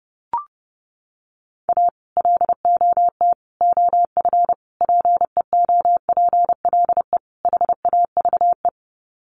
Morse code, heard as a keyed tone, about 30 wpm, 700 Hz. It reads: A LOT OF PEOPLE HAVE
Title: Sets of 5 Words 30wpm